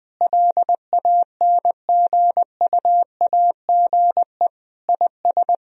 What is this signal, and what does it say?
Morse code, heard as a keyed tone, about 21 wpm, 700 Hz. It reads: LANGUAGE IS
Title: Sentences from Top 1000 Words - Review Part 3 - 20wpm